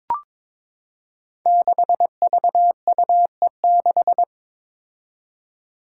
Morse code, heard as a keyed tone, about 22 wpm, 700 Hz. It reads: 6VUE6
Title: Five Challenging Letter-Number Combinations 22wpm